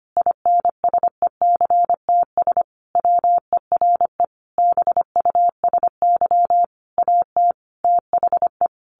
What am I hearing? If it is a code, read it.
Morse code, heard as a keyed tone, about 25 wpm, 700 Hz. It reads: INSECTS WERE BUSY AT THE
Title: A Sound Of Thunder 25wpm